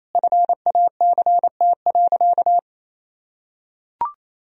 Morse code, heard as a keyed tone, about 27 wpm, 700 Hz. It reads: FACT.
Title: Sentences from Top 1000 Words - Review Part 3 - 28wpm